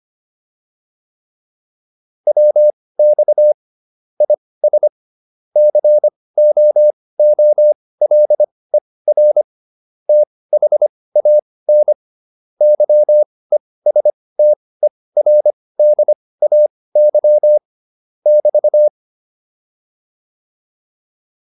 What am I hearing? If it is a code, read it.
Morse code, heard as a keyed tone, about 25 wpm, spaced as for 18 wpm, 600 Hz. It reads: WX IS COOLER THAN YESTERDAY <BT>
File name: QSO409 18